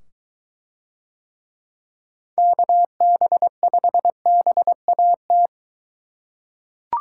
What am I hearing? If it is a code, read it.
Morse code, heard as a keyed tone, about 21 wpm, 700 Hz. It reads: KB5BAT